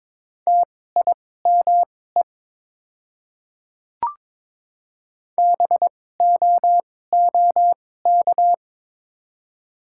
Morse code, heard as a keyed tone, about 22 wpm, spaced as for 16 wpm, 690 Hz. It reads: TIME BOOK
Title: 2x Character Spacing - Top 500 Words - 22wpm